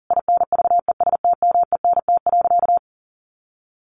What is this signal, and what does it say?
Morse code, heard as a keyed tone, about 40 wpm, 700 Hz. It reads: INVESTMENT.
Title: Sentences from Top 700 Words 40wpm